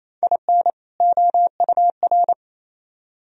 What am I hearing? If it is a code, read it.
Morse code, heard as a keyed tone, about 35 wpm, 700 Hz. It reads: IN OUR
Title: Speed-Racing - Sets of 2 Words 28wpm